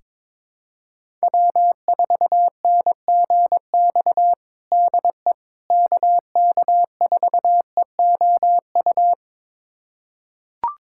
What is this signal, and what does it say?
Morse code, heard as a keyed tone, about 22 wpm, 700 Hz. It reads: W4NGX DE KK4EOU
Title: USA Callsign DE Callsign 22wpm